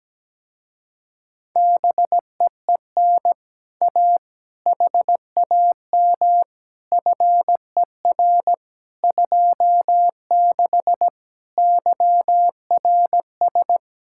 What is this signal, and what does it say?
Morse code, heard as a keyed tone, about 17 wpm, 700 Hz. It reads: BEEN A HAM FER 26 YRS
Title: Speed-Racing Common QSO Phrases Encore - 17wpm